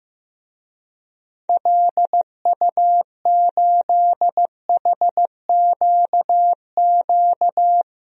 Morse code, heard as a keyed tone, about 15 wpm, 700 Hz. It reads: LU8HQQ
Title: DXCC-100 Call Signs - 6 Characters - 15wpm